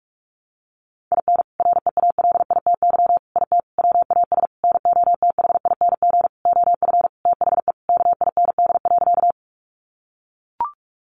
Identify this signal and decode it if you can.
Morse code, heard as a keyed tone, about 44 wpm, 700 Hz. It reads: IN REALITY IT WAS NOTHING OF THE KIND.
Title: Sentences from Top 1200 Words 45wpm